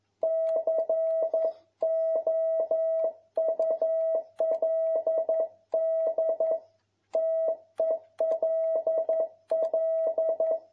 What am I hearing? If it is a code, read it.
Morse code, heard as a keyed tone, about 11 wpm, 650 Hz. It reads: COULD TELL
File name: jh8cbh / 2021cw